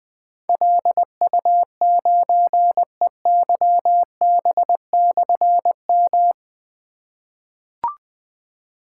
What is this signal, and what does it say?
Morse code, heard as a keyed tone, about 21 wpm, 700 Hz. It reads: LU9EYB/M
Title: Difficult Call Signs 20wpm